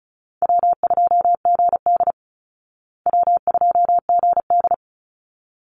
Morse code, heard as a keyed tone, about 34 wpm, 700 Hz. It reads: W2GD W2GD
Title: Easy Call Signs Repeated 2x - 35wpm